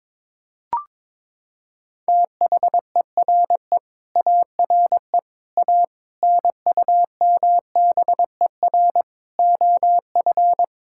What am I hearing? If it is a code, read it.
Morse code, heard as a keyed tone, about 22 wpm, 700 Hz. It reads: THERE ARE A NUMBER OF
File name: practice / 22wpm / Sets of 8 Words - 22wpm